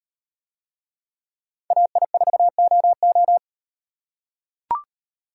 Morse code, heard as a keyed tone, about 38 wpm, 700 Hz. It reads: AI4OO